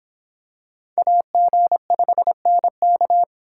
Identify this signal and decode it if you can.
Morse code, heard as a keyed tone, about 25 wpm, 700 Hz. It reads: AG5NK